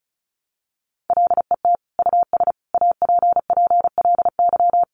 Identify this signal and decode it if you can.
Morse code, heard as a keyed tone, about 34 wpm, 700 Hz. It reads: LET US APPLY
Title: Sentences from Top 1200 Words 35wpm